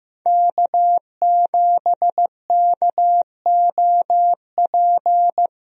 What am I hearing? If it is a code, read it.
Morse code, heard as a keyed tone, about 15 wpm, 700 Hz. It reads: K7KOP
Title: USA Call Signs - 5 Characters - Repeated 2x - 15wpm